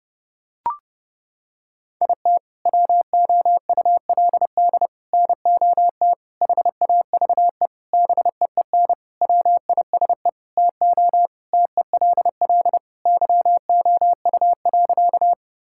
Morse code, heard as a keyed tone, about 30 wpm, 700 Hz. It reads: IT WOULD NOT HAVE BEEN WISE TO TELL YOU.
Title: Sentences from Top 1600 Words 30wpm